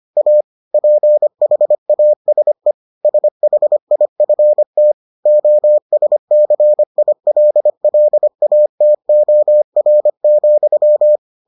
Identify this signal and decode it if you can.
Morse code, heard as a keyed tone, about 25 wpm, 600 Hz. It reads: A PHASE SHIFT OSCILLATOR,